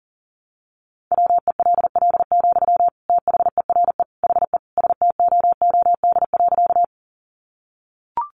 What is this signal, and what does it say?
Morse code, heard as a keyed tone, about 40 wpm, 700 Hz. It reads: WELL, THERE HE STOOD.
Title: Sentences from Top 500 Words 40wpm